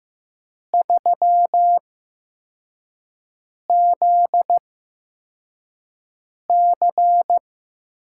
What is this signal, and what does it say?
Morse code, heard as a keyed tone, about 15 wpm, spaced as for 4 wpm, 700 Hz. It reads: 3ZC
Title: USA Call Signs - 6 Characters - 8x Character Spacing - 15wpm